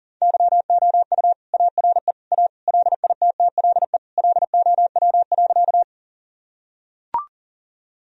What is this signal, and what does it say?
Morse code, heard as a keyed tone, about 40 wpm, 700 Hz. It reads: YOU ARE A LITTLE LOW.
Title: Sentences from Top 1000 Words - Review Part 1 - 40wpm